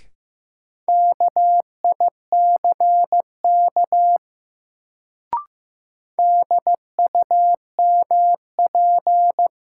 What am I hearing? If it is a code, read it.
Morse code, heard as a keyed tone, about 15 wpm, 700 Hz. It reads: KICK DUMP
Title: Sentences from Top 1300 Words - New Word Drill Alt - 15wpm